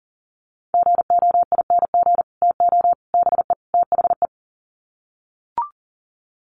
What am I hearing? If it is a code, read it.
Morse code, heard as a keyed tone, about 40 wpm, 700 Hz. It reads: GOING TO BE THE
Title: Sets of 4 Words 40wpm